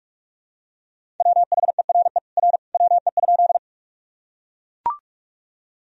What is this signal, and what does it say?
Morse code, heard as a keyed tone, about 41 wpm, 700 Hz. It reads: WHERE R WE?
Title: Sentences from Top 200 Words and Abbreviations 45wpm